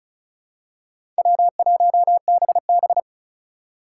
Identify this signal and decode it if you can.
Morse code, heard as a keyed tone, about 34 wpm, 700 Hz. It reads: W1BB
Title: USA Call Signs - 4 Characters - 4x Character Spacing - 35wpm